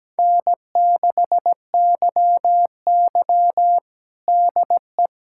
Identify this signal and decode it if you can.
Morse code, heard as a keyed tone, about 17 wpm, 700 Hz. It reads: N6YY DE
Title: USA Callsign DE Callsign - 4 Characters - 17wpm